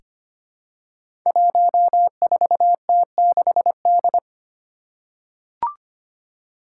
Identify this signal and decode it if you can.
Morse code, heard as a keyed tone, about 25 wpm, 700 Hz. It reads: 14T6D
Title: Five Challenging Letter-Number Combinations 25wpm